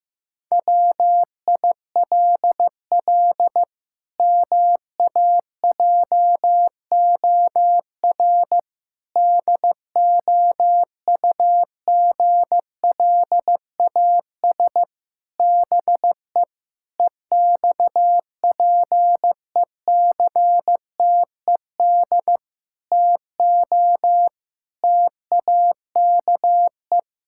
Morse code, heard as a keyed tone, about 15 wpm, 700 Hz. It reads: WILL MAJOR DOUGLAS BE EXPECTED TO TAKE
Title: Pangram Challenge v2 - 15wpm